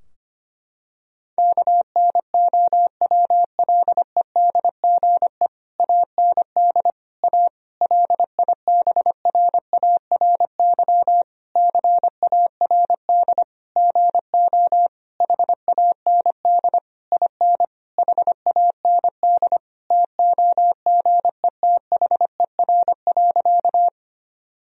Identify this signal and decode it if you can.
Morse code, heard as a keyed tone, about 25 wpm, 700 Hz. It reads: KNOWLEDGE AND A LIBRARY CARD GO HAND IN HAND TOGETHER.
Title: Sentences from Top 1000 Words - Review Part 4 - 25wpm